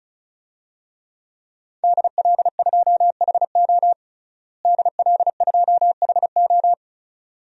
Morse code, heard as a keyed tone, about 34 wpm, 700 Hz. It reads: DL2HO DL2HO
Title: DXCC-100 Call Signs - 5 Characters - Repeated 2x - 35wpm